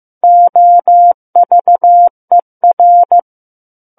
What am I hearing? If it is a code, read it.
Morse code, heard as a keyed tone, about 15 wpm, 700 Hz. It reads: OVER
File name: Book One / 06-Chapter 6 - 15 WPM